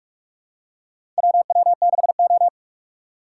Morse code, heard as a keyed tone, about 46 wpm, 700 Hz. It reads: WW6O
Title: USA Call Signs - 4 Characters - 2x Character Spacing - 45wpm